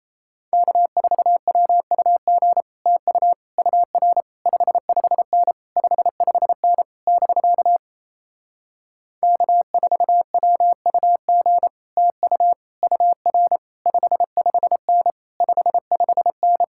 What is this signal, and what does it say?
Morse code, heard as a keyed tone, about 33 wpm, 700 Hz. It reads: K4WUG TU UR 55N 55N <BK> K4WUG TU UR 55N 55N
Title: Speed-Racing Common QSO Phrases - 28wpm